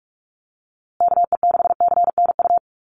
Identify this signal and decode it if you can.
Morse code, heard as a keyed tone, about 44 wpm, 700 Hz. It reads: KE6CNU
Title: USA Call Signs Repeated 2x 45wpm